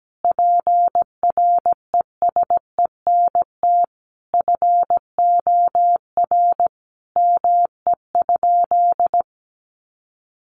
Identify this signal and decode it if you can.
Morse code, heard as a keyed tone, about 17 wpm, 700 Hz. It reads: PRESENT FOR ME?
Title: Sentences from Top 800 Words 17wpm